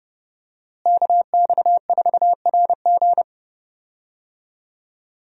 Morse code, heard as a keyed tone, about 30 wpm, 700 Hz. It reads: KX4RG
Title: USA Call Signs - 5 Characters - Repeated 2x - 30wpm